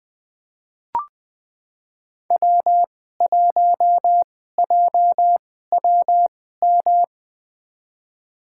Morse code, heard as a keyed tone, about 20 wpm, spaced as for 14 wpm, 700 Hz. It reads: W1JWM
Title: USA Call Signs - 2x Character Spacing - 20wpm